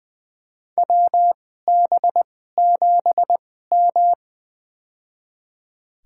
Morse code, heard as a keyed tone, about 20 wpm, spaced as for 15 wpm, 700 Hz. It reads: WB7M
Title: USA Call Signs - 4 Characters - 2x Character Spacing - 20wpm